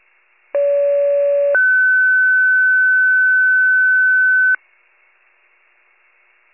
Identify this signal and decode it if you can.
Two-tone page, heard as a one-second tone, then a three-second tone, about 570 then 1500 Hz. It is heard from 0.5 to 4.6 s.
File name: wcalert